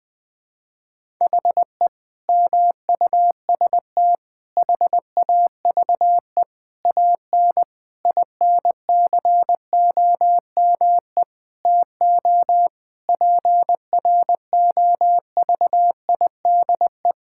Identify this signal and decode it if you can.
Morse code, heard as a keyed tone, about 20 wpm, 700 Hz. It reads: HE MUST HAVE AN INCOME TO PROVIDE
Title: Sentences from Top 1000 Words - Review Part 5 - 20wpm